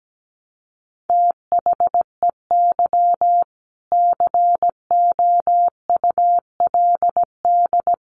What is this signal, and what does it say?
Morse code, heard as a keyed tone, about 17 wpm, 700 Hz. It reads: THEY COULD
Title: Sentences from Top 1100 Words 17wpm